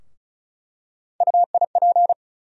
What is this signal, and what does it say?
Morse code, heard as a keyed tone, about 34 wpm, 700 Hz. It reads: UIP
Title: USA Call Sign Postfixes - 3 Letters 35wpm